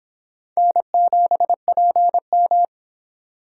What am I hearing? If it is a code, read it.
Morse code, heard as a keyed tone, about 25 wpm, 700 Hz. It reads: N7PM